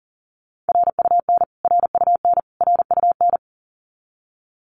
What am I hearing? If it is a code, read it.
Morse code, heard as a keyed tone, about 40 wpm, 700 Hz. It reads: RUN RUN RUN
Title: Top 500 Words Repeated 40wpm